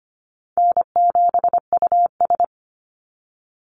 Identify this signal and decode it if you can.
Morse code, heard as a keyed tone, about 25 wpm, 700 Hz. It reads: N7US
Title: Easy Call Signs - 25wpm